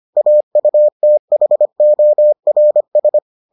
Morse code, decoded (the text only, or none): AUTHORS